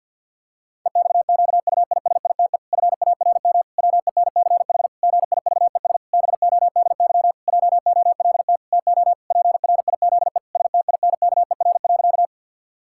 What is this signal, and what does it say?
Morse code, heard as a keyed tone, about 52 wpm, 700 Hz. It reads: EXQUISITE FARM WENCH GIVES BODY JOLT TO PRIZE STINKER.
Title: Pangram Challenge v2 - 50wpm